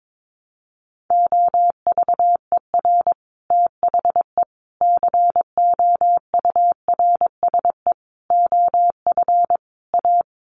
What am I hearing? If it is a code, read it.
Morse code, heard as a keyed tone, about 22 wpm, 700 Hz. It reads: OVER THE COURSE OF A
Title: Sets of 5 Words Encore 22wpm